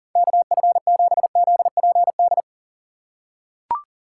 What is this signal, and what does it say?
Morse code, heard as a keyed tone, about 40 wpm, 700 Hz. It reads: KF7ZPD
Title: USA Call Signs - 4x Character Spacing - 40wpm